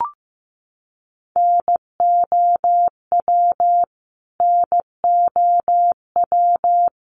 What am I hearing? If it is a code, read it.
Morse code, heard as a keyed tone, about 15 wpm, 700 Hz. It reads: NOW NOW
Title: Top 500 Words Repeated 15wpm